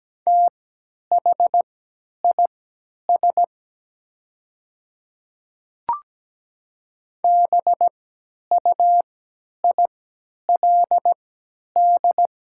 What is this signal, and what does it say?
Morse code, heard as a keyed tone, about 17 wpm, spaced as for 10 wpm, 700 Hz. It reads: THIS BUILD
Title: 3x Character Spacing - Top 200 Words - 17wpm